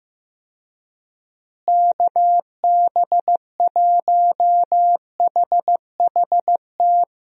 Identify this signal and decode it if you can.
Morse code, heard as a keyed tone, about 15 wpm, 700 Hz. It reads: KB1HHT
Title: USA Call Signs - 6 Characters - Repeated 3x - 15wpm